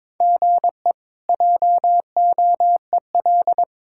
Morse code, heard as a keyed tone, about 22 wpm, 700 Hz. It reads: GE JOEL
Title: K1USN SST QSO Practice - 22wpm